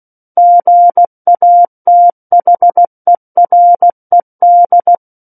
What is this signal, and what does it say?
Morse code, heard as a keyed tone, about 16 wpm, 700 Hz. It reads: GATHERED
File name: Book One / 07-Chapter 7 - 16 WPM